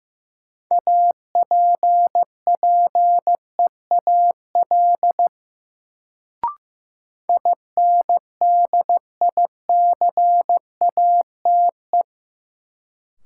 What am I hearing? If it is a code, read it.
Morse code, heard as a keyed tone, about 15 wpm, 700 Hz. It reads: APPEAL INDICATE
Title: Sentences from Top 1300 Words - New Word Drill - 15wpm